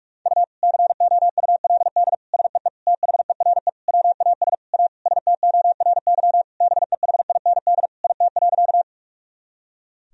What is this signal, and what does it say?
Morse code, heard as a keyed tone, about 47 wpm, 700 Hz. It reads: U COULD SEE THERE WAS A STORY BEHIND IT.
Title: Sentences from Top 500 Words and Abbreviations 45wpm